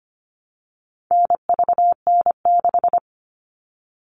Morse code, heard as a keyed tone, about 25 wpm, 700 Hz. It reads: NVN6
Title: All Cut Numbers - 4-Digit 25wpm